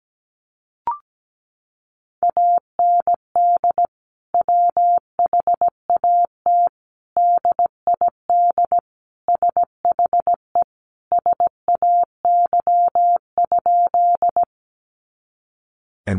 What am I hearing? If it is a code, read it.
Morse code, heard as a keyed tone, about 17 wpm, 700 Hz. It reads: AND WHAT DID SHE SAY?
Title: Sentences from Top 200 Words 17wpm